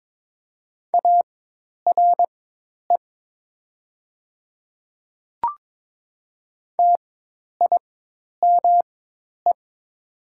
Morse code, heard as a keyed tone, about 22 wpm, spaced as for 10 wpm, 700 Hz. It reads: ARE TIME
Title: 4x Character Spacing - Top 100 Words - 22wpm